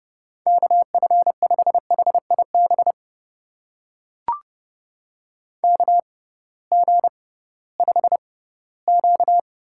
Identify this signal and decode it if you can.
Morse code, heard as a keyed tone, about 32 wpm, spaced as for 18 wpm, 700 Hz. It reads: KF5HIB KG5Q
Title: USA Call Signs - 6 Characters - 6x Character Spacing - 30wpm